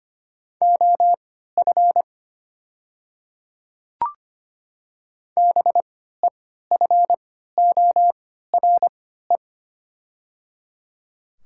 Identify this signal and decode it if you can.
Morse code, heard as a keyed tone, about 25 wpm, spaced as for 14 wpm, 700 Hz. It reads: OF BEFORE
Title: 3x Character Spacing - Top 200 Words - 25wpm